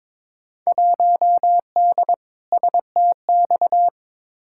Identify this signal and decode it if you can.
Morse code, heard as a keyed tone, about 22 wpm, 700 Hz. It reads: 1D STX
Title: Field Day Exchanges - 22wpm